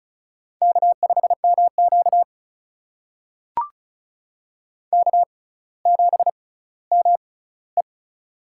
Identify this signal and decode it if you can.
Morse code, heard as a keyed tone, about 36 wpm, spaced as for 18 wpm, 690 Hz. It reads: K5MQ K7ME
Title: USA Call Signs - 4 Characters - 6x Character Spacing - 35wpm